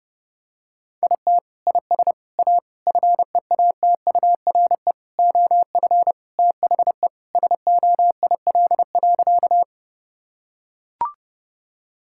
Morse code, heard as a keyed tone, about 31 wpm, 700 Hz. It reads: IT IS A FEATURE OF THE SOIL.
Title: Sentences from Top 1000 Words - Review Part 5 - 30wpm